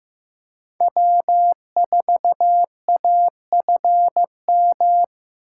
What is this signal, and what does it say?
Morse code, heard as a keyed tone, about 15 wpm, 700 Hz. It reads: W4AFM
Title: USA Call Signs - 5 Characters - 6x Character Spacing - 15wpm